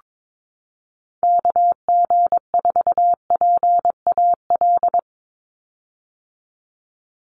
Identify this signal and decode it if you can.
Morse code, heard as a keyed tone, about 22 wpm, 700 Hz. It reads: KG4PAL
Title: USA Call Signs 22wpm